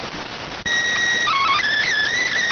SSTV (slow-scan television), PD 290.